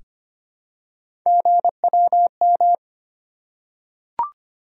Morse code, heard as a keyed tone, about 25 wpm, 700 Hz. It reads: GWM
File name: Three Letter-Number Combinations - 25wpm